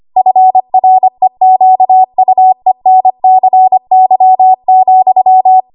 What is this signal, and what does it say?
Morse code, heard as a keyed tone, about 25 wpm, 750 Hz. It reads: FREQUENCY,
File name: Morse / 25WPM